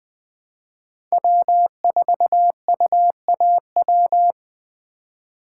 Morse code, heard as a keyed tone, about 20 wpm, 700 Hz. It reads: W4UAW
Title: USA Call Signs - 3x Character Spacing - 20wpm